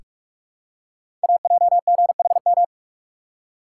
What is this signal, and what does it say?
Morse code, heard as a keyed tone, about 44 wpm, 690 Hz. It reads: AJGHM
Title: Code Groups - 5 Letters - 45wpm